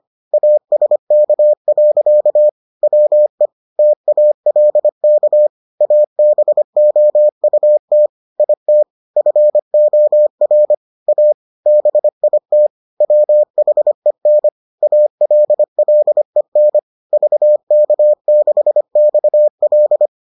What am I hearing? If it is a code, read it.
Morse code, heard as a keyed tone, about 25 wpm, 600 Hz. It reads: ASK. WE TALK ABOUT IT FOR A BIT WHEN ALLEN VK6XL